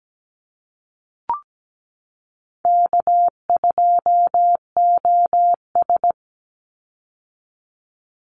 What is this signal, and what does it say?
Morse code, heard as a keyed tone, about 17 wpm, 700 Hz. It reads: K2OS
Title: Easy Call Signs - 17wpm